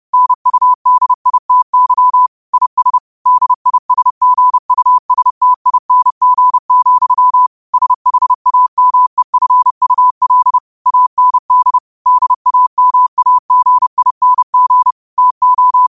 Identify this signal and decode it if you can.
Morse code, heard as a keyed tone, about 30 wpm, 1000 Hz. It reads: NUDITY IS DISGUSTING, SHAMEFUL AND DAMAGING TO